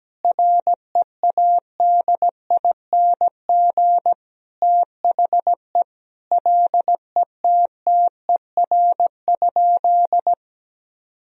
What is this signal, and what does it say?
Morse code, heard as a keyed tone, about 17 wpm, 700 Hz. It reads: READING THE LETTER?
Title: Sentences from Top 1000 Words - Review Part 3 - 17wpm